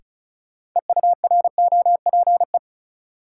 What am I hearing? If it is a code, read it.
Morse code, heard as a keyed tone, about 36 wpm, 700 Hz. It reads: EUROPE
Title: Sets of 5 Words Encore - New Word Drill - 35wpm